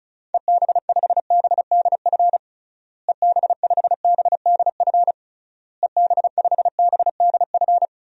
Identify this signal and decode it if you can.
Morse code, heard as a keyed tone, about 34 wpm, 700 Hz. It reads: EB5BDF EB5BDF EB5BDF
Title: DXCC-100 Call Signs - 6 Characters - Repeated 3x - 35wpm